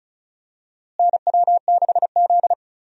Morse code, heard as a keyed tone, about 36 wpm, 700 Hz. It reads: NW6Z
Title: USA Call Signs - 4 Characters - Repeated 3x - 35wpm